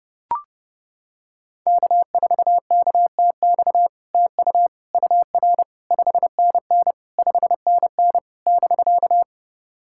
Morse code, heard as a keyed tone, about 30 wpm, 700 Hz. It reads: K4KTX TU UR 5NN 5NN <BK>
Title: SOTA QSO Practice 30wpm